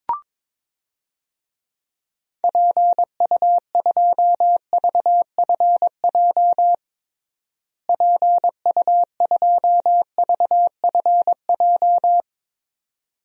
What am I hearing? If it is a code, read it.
Morse code, heard as a keyed tone, about 22 wpm, 700 Hz. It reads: PU2VFJ PU2VFJ
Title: DXCC-100 Call Signs - 6 Characters - Repeated 2x - 22wpm